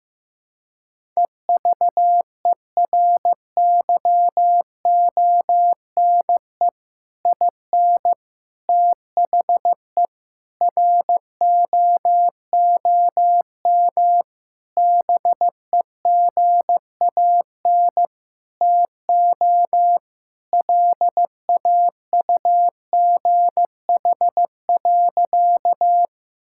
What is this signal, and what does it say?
Morse code, heard as a keyed tone, about 15 wpm, 700 Hz. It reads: EVERYONE IN THE ROOM BEGAN TO LAUGH.
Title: Sentences from Top 1000 Words - Review Part 5 - 15wpm